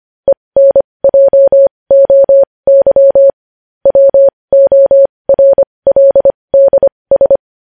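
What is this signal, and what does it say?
Morse code, heard as a keyed tone, about 25 wpm, spaced as for 20 wpm, 550 Hz. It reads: ENJOY WORLDS
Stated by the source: Morse code practice sending news headlines at 20 wpm.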